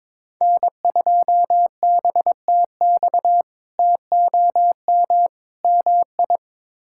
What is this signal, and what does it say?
Morse code, heard as a keyed tone, about 22 wpm, 700 Hz. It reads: N2BTX TOM MI
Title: K1USN SST QSO Practice - 22wpm